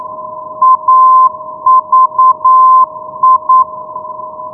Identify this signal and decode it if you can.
Morse code, heard as a keyed tone, about 9 wpm, 1040 Hz. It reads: AVI
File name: europe / i-665 ndb aviano avi-390 rb 2010 ve